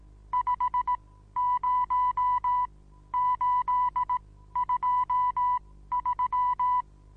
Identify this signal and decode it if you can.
Morse code, heard as a keyed tone, about 18 wpm, spaced as for 13 wpm, 990 Hz. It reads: 50823
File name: Onlinelehrgang / morsen / DJ4UF Morsen25